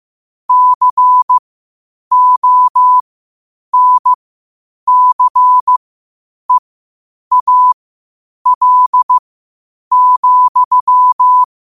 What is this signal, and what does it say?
Morse code, heard as a keyed tone, about 15 wpm, spaced as for 8 wpm, 1000 Hz. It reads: CONCEAL,
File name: qotdfiles / 8wpm